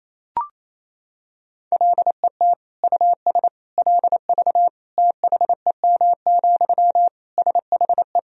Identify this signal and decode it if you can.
Morse code, heard as a keyed tone, about 28 wpm, 700 Hz. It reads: LET US LV THEM, SHE
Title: Sentences from Top 400 Words and Abbreviations 28wpm